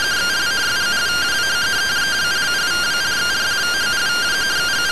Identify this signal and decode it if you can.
Two-tone teleprinter signal, about 170 Hz shift.